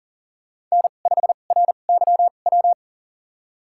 Morse code, heard as a keyed tone, about 40 wpm, spaced as for 29 wpm, 700 Hz. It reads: N5RYW
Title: USA Call Signs - 2x Character Spacing - 40wpm